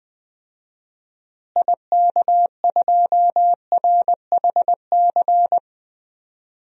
Morse code, heard as a keyed tone, about 20 wpm, 690 Hz. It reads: IK2RHC
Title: DXCC-100 Call Signs - 6 Characters - Repeated 2x - 20wpm